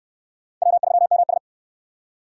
Morse code, heard as a keyed tone, about 60 wpm, 700 Hz. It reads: W3MS